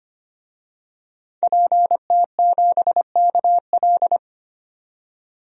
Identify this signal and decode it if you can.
Morse code, heard as a keyed tone, about 25 wpm, 700 Hz. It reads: PT7KL
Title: DXCC-100 Call Signs - 5 Characters - 25wpm